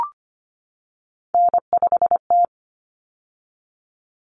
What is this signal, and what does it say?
Morse code, heard as a keyed tone, about 25 wpm, 700 Hz. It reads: N5T
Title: Common Cut Numbers - 3-Digit 25wpm